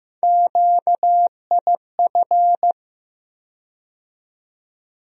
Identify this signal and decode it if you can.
Morse code, heard as a keyed tone, about 15 wpm, 700 Hz. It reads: QIF
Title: USA Call Sign Postfixes - 3 Letters 15wpm